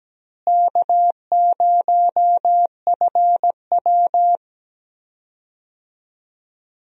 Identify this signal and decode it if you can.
Morse code, heard as a keyed tone, about 17 wpm, 700 Hz. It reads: K0FW
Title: USA Call Signs - 4 Characters - Repeated 2x - 17wpm